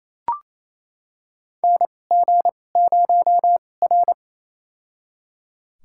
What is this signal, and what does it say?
Morse code, heard as a keyed tone, about 28 wpm, spaced as for 20 wpm, 700 Hz. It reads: NG0R
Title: USA Call Signs - 4 Characters - 2x Character Spacing - 28wpm